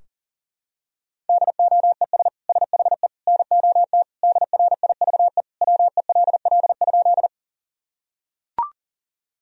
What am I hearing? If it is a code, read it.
Morse code, heard as a keyed tone, about 40 wpm, 700 Hz. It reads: DOES SHE NOT DRIVE WELL?
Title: Sentences from Top 1000 Words - Review Part 2 - 40wpm